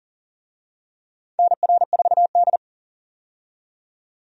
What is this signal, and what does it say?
Morse code, heard as a keyed tone, about 40 wpm, 700 Hz. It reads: NR4D
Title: USA Call Signs - 4 Characters - 40wpm